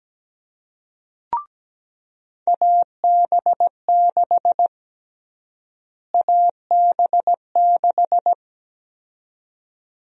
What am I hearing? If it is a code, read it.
Morse code, heard as a keyed tone, about 17 wpm, 700 Hz. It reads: AB6 AB6
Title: USA Call Sign Prefixes - Repeated 2x - 17wpm